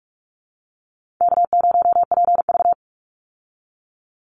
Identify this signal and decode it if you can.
Morse code, heard as a keyed tone, about 44 wpm, 700 Hz. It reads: K0PV
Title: Call Signs Repeated 2x - 45wpm